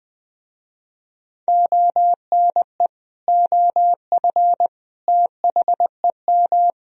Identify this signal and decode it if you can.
Morse code, heard as a keyed tone, about 20 wpm, 700 Hz. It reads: ONE OF THEM